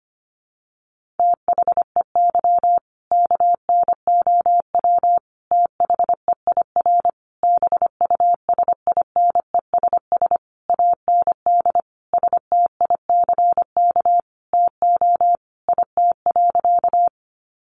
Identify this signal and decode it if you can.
Morse code, heard as a keyed tone, about 25 wpm, 700 Hz. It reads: THEY KNOW THEIR BUSINESS AND STICK TO IT.
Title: Sentences from Top 1200 Words 25wpm